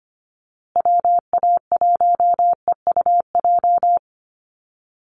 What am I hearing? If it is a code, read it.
Morse code, heard as a keyed tone, about 25 wpm, 700 Hz. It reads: WA1EUJ
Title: USA Call Signs Repeated 3x 25wpm